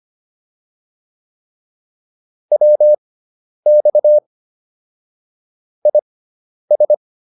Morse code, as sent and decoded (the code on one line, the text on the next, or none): .-- -..- / .. ...
WX IS